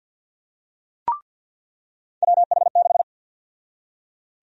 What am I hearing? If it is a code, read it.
Morse code, heard as a keyed tone, about 48 wpm, 700 Hz. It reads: WH6
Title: USA Call Sign Prefixes - 50wpm